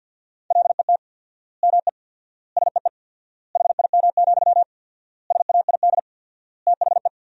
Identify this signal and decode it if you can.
Morse code, heard as a keyed tone, about 50 wpm, 700 Hz. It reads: LET ME SEE HIM, SAID THE
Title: 4x Word-Spacing - Sentences from Top 400 Words - 50wpm